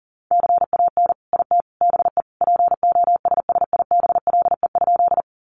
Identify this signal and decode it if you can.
Morse code, heard as a keyed tone, about 38 wpm, 700 Hz. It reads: CAN IT BE POSSIBLE?
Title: Sentences from Top 500 Words 40wpm